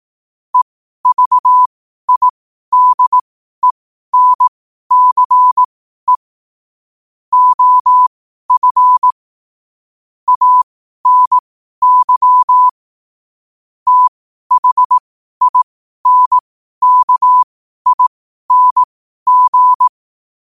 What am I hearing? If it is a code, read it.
Morse code, heard as a keyed tone, about 18 wpm, spaced as for 12 wpm, 1000 Hz. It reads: EVIDENCE OF ANY THINKING
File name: qotdfiles / 12wpm